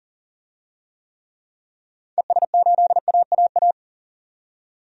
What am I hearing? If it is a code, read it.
Morse code, heard as a keyed tone, about 40 wpm, 700 Hz. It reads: ES8AAA
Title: DXCC-100 Call Signs - 6 Characters - 40wpm